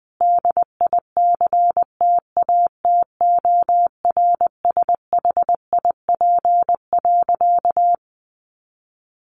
Morse code, heard as a keyed tone, about 20 wpm, 700 Hz. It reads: DICTATORSHIP.
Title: A Sound Of Thunder 20wpm